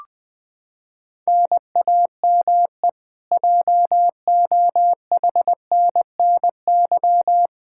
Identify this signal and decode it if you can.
Morse code, heard as a keyed tone, about 21 wpm, 700 Hz. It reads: NAME JOHNNY
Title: Practice QSO - Exchange 1 - Beginning - 20wpm